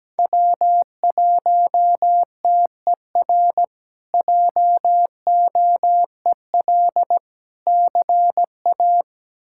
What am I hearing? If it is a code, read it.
Morse code, heard as a keyed tone, about 17 wpm, 700 Hz. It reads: W1TER JOEL CA
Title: K1USN SST QSO Practice - 17wpm